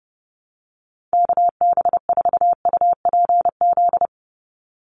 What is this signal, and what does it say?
Morse code, heard as a keyed tone, about 30 wpm, 700 Hz. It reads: KB4UPZ
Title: USA Call Signs 30wpm